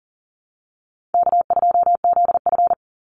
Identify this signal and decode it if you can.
Morse code, heard as a keyed tone, about 40 wpm, 700 Hz. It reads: K2ZF
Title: Easy Call Signs - 40wpm